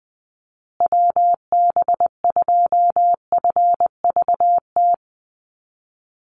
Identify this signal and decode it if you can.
Morse code, heard as a keyed tone, about 20 wpm, 700 Hz. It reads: WB2FVT
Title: USA Call Signs Repeated 3x 20wpm